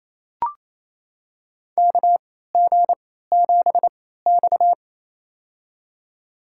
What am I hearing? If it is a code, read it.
Morse code, heard as a keyed tone, about 27 wpm, spaced as for 16 wpm, 700 Hz. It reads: KG7X
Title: USA Call Signs - 4 Characters - 3x Character Spacing - 28wpm